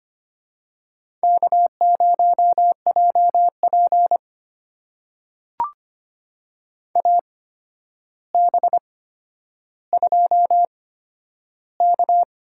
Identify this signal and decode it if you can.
Morse code, heard as a keyed tone, about 25 wpm, spaced as for 11 wpm, 700 Hz. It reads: K0JP AB2K
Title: USA Call Signs - 4 Characters - 8x Character Spacing - 25wpm